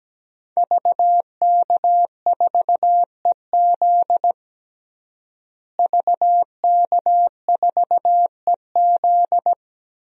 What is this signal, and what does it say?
Morse code, heard as a keyed tone, about 17 wpm, 700 Hz. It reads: VK4EZ VK4EZ
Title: DXCC-100 Call Signs - 5 Characters - Repeated 3x - 17wpm